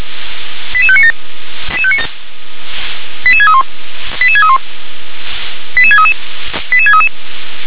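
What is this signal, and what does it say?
Five-tone selective call (ZVEI), heard as quick runs of five tones, tones of about 70 ms. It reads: 88597 88631 88631 88633 88633
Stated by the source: ZVEI five-tone selective call 88732, received by SDR.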